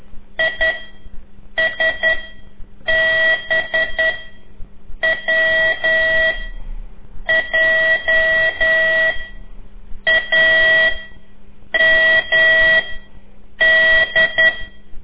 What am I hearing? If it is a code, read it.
Morse code, heard as a keyed tone, about 10 wpm, 1950 Hz. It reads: I S B W J A M D
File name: CORSI CW